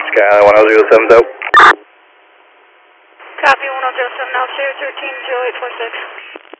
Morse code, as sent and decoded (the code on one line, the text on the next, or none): none